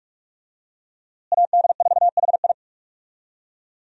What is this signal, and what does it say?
Morse code, heard as a keyed tone, about 42 wpm, 700 Hz. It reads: AD4HI
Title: USA Call Signs - 5 Characters - Repeated 3x - 45wpm